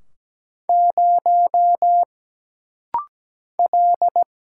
Morse code, heard as a keyed tone, about 17 wpm, 700 Hz. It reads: 0 L
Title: Single Letter-Number Alt - Rapid-Fire 17wpm